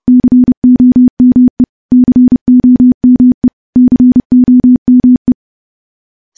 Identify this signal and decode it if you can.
Morse code, heard as a keyed tone, about 30 wpm, 250 Hz. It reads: COME COME COME
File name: morse / morse-training-30WPM